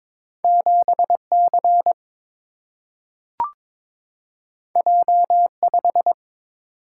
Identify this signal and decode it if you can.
Morse code, heard as a keyed tone, about 22 wpm, 700 Hz. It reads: 7 C J 5
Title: Single-Letter Single-Number 22wpm